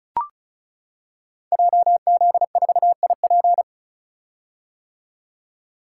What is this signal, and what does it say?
Morse code, heard as a keyed tone, about 36 wpm, 700 Hz. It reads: JZ4IP
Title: Code Groups - 5 Letter:Numbers - 35wpm